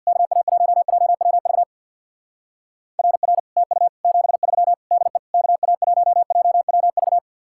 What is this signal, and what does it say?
Morse code, heard as a keyed tone, about 64 wpm, 700 Hz. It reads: KA1JWV RR TU 73 DE KA1JWV